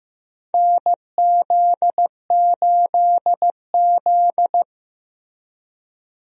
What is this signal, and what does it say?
Morse code, heard as a keyed tone, about 15 wpm, 700 Hz. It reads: NZ8Z
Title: USA Call Signs - 4 Characters - Repeated 2x - 15wpm